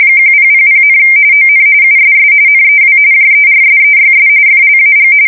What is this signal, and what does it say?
RTTY, about 45.45 baud. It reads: 1000, ON USS NEW JERSEYS FLAG
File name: fsk170 / battleshipnj